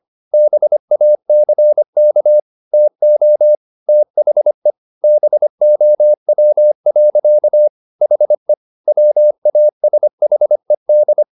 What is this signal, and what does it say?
Morse code, heard as a keyed tone, about 25 wpm, 600 Hz. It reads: BACK TO THE BOW. HE WASHED